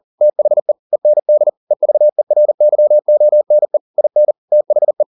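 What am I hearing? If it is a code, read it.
Morse code, heard as a keyed tone, about 40 wpm, 600 Hz. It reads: THE END EVERYONE IN THE
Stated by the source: Morse code at 40 wpm, practice text, CW med Gnister.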